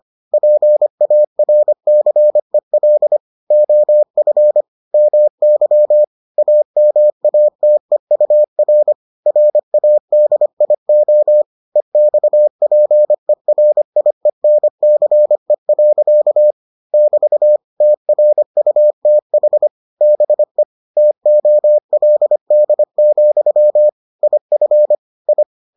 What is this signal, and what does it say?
Morse code, heard as a keyed tone, about 25 wpm, 600 Hz. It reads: PARCEL OF MY AMATEUR RADIO EXPERIENCE. <BT> TRUTH BE TOLD, IF I